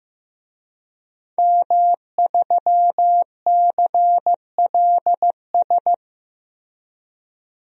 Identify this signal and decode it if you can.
Morse code, heard as a keyed tone, about 15 wpm, 700 Hz. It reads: M3CLS
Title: DXCC-100 Call Signs - 5 Characters - 15wpm